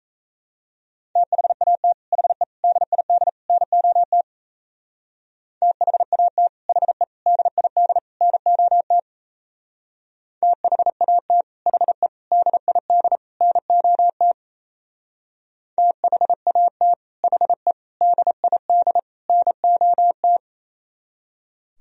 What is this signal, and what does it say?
Morse code, heard as a keyed tone, about 33 wpm, 700 Hz. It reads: THAT HE DID NOT THAT HE DID NOT THAT HE DID NOT THAT HE DID NOT
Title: Speed-Racing - Sets of 4 Words 28wpm